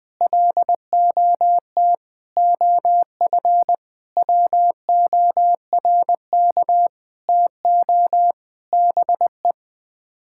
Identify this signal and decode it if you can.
Morse code, heard as a keyed tone, about 20 wpm, 700 Hz. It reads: LOT OF WORK TO BE
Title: Sets of 9 Words - 20wpm